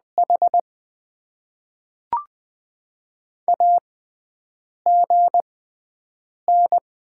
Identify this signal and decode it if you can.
Morse code, heard as a keyed tone, about 20 wpm, spaced as for 7 wpm, 690 Hz. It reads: H AGN
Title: Top 20 QSO Elements - 6x Character Spacing - 20wpm